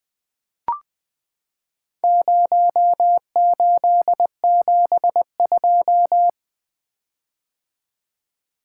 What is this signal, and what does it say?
Morse code, heard as a keyed tone, about 20 wpm, 700 Hz. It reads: 0872
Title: Four Letter-Number Combinations - 20wpm